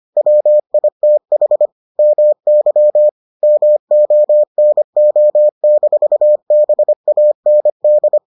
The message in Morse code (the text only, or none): WITH MY MONO-BAND